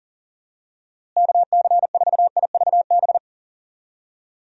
Morse code, heard as a keyed tone, about 40 wpm, 700 Hz. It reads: KC4IVB
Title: USA Call Signs - 6 Characters - Repeated 3x - 40wpm